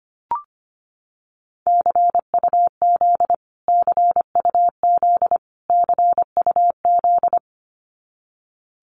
Morse code, heard as a keyed tone, about 25 wpm, 700 Hz. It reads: CUZ CUZ CUZ
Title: Abbreviations Repeated 3x - 25wpm